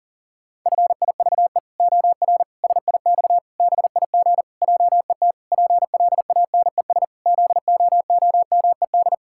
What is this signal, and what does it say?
Morse code, heard as a keyed tone, about 40 wpm, 700 Hz. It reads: FIVE OR SIX BIG JET PLANES ZOOMED
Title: Pangram Challenge v2 - 40wpm